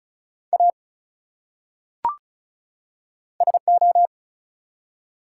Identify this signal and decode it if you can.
Morse code, heard as a keyed tone, about 36 wpm, 700 Hz. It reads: A SO
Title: Top 180 QSO Elements 35wpm